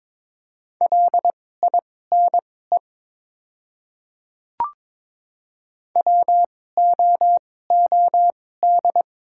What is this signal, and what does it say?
Morse code, heard as a keyed tone, about 22 wpm, spaced as for 16 wpm, 690 Hz. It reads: LINE WOOD
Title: 2x Character Spacing - Top 300 Words - 22wpm